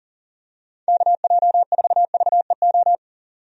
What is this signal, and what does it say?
Morse code, heard as a keyed tone, about 40 wpm, 700 Hz. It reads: KJ4VEO
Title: USA Call Signs - 6 Characters - 40wpm